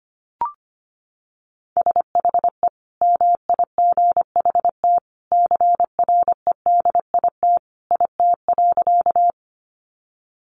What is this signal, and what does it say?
Morse code, heard as a keyed tone, about 25 wpm, 700 Hz. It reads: SHE MIGHT CREDIT IT.
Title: Sentences from Top 1000 Words 25wpm